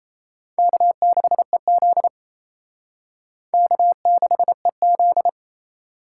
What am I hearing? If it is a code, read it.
Morse code, heard as a keyed tone, about 38 wpm, 700 Hz. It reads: K6EZ K6EZ